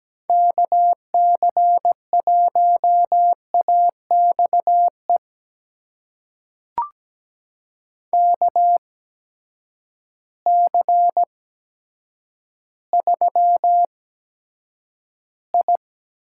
Morse code, heard as a keyed tone, about 17 wpm, spaced as for 8 wpm, 700 Hz. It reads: KC1AXE KC3I
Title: USA Call Signs - 6 Characters - 8x Character Spacing - 17wpm